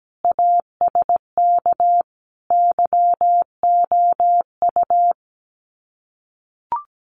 Morse code, heard as a keyed tone, about 17 wpm, 700 Hz. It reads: ASK YOU
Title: Sets of 5 Words Encore2 17wpm